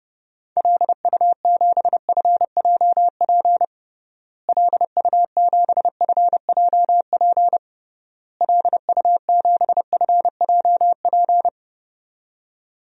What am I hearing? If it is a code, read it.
Morse code, heard as a keyed tone, about 30 wpm, 700 Hz. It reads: LU7FJP LU7FJP LU7FJP
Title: DXCC-100 Call Signs - 6 Characters - Repeated 3x - 30wpm